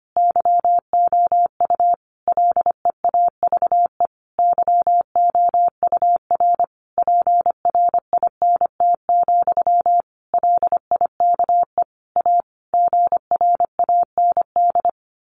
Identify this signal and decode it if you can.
Morse code, heard as a keyed tone, about 25 wpm, 700 Hz. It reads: YOU LEAVE YOUR PRINT, LIKE A GRAND
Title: A Sound Of Thunder 25wpm